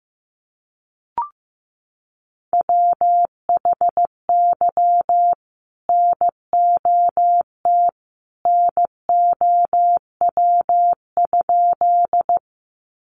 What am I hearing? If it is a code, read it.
Morse code, heard as a keyed tone, about 15 wpm, 700 Hz. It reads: WHY NOT NOW?
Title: Sentences from Top 600 Words 15wpm